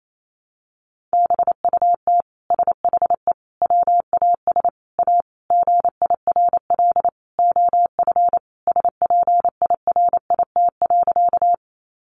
Morse code, heard as a keyed tone, about 28 wpm, 700 Hz. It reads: BUT SHE WAS A GIRL OF SPIRIT.
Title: Sentences from Top 1400 Words 28wpm